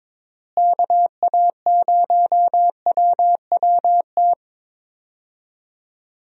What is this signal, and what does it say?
Morse code, heard as a keyed tone, about 21 wpm, 700 Hz. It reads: KA0WWT
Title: USA Call Signs - 6 Characters - Repeated 3x - 22wpm